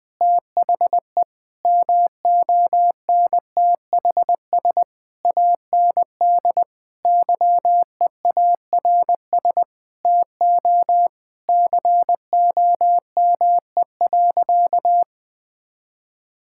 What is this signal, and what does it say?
Morse code, heard as a keyed tone, about 20 wpm, 700 Hz. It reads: THE MONTHS AND YEARS TO COME.
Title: Sentences with 7-Word Phrases - Spoken Beginning - 20wpm